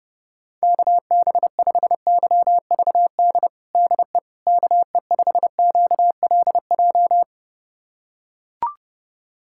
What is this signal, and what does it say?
Morse code, heard as a keyed tone, about 30 wpm, 700 Hz. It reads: KB5YVD DE KE5QLJ
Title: USA Callsign DE Callsign - 6 Characters - 30wpm